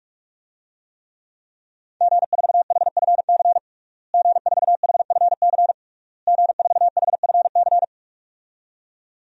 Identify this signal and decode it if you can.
Morse code, heard as a keyed tone, about 46 wpm, 700 Hz. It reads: G4HFC G4HFC G4HFC
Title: DXCC-100 Call Signs - 5 Characters - Repeated 3x - 45wpm